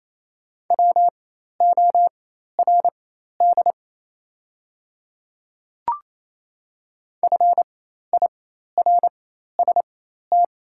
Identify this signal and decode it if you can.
Morse code, heard as a keyed tone, about 28 wpm, spaced as for 13 wpm, 700 Hz. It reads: WORD FIRST
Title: 4x Character Spacing - Top 100 Words - 28wpm